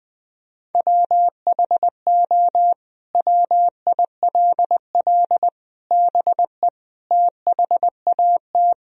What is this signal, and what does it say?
Morse code, heard as a keyed tone, about 20 wpm, 700 Hz. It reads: WHO WILL BE THAT
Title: Sentences from Top 1000 Words - Review Part 4 - 20wpm